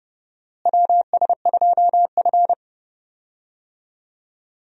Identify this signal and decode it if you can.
Morse code, heard as a keyed tone, about 32 wpm, 700 Hz. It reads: WS2F
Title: USA Call Signs - 4 Characters - Repeated 3x - 30wpm